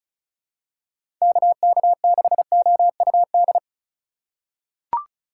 Morse code, heard as a keyed tone, about 36 wpm, 700 Hz. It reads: KK6OUD
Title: USA Call Signs - 6 Characters - 3x Character Spacing - 35wpm